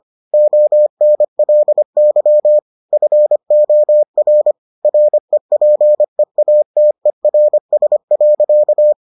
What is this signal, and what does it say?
Morse code, heard as a keyed tone, about 25 wpm, 600 Hz. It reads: ONLY FOR REPEATERS.